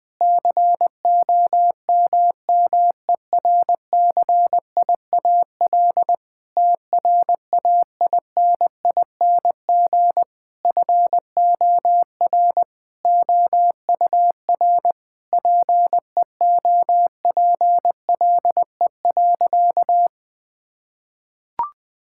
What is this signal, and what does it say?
Morse code, heard as a keyed tone, about 20 wpm, 700 Hz. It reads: COMMERCIAL TRAINING FOR OUR PEOPLE.
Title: Sentences from Top 1000 Words - Review Part 5 - 20wpm